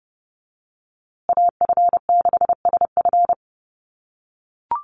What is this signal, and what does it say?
Morse code, heard as a keyed tone, about 30 wpm, 700 Hz. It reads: AF6SF
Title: USA Call Signs Repeated 3x 30wpm